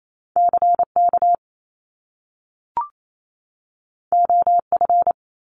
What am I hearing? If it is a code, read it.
Morse code, heard as a keyed tone, about 28 wpm, 700 Hz. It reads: CK OF
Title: Top Two Letter Combinations in Words 28wpm